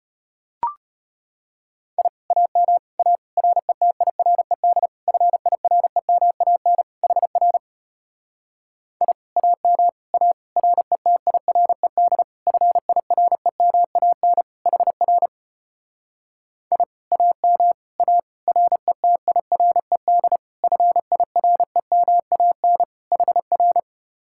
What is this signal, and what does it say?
Morse code, heard as a keyed tone, about 33 wpm, 700 Hz. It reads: I AM A RETIRED FIREMAN HR I AM A RETIRED FIREMAN HR I AM A RETIRED FIREMAN HR